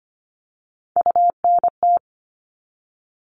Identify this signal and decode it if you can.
Morse code, heard as a keyed tone, about 25 wpm, 700 Hz. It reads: UNT
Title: All Cut Numbers - 3-Digit 25wpm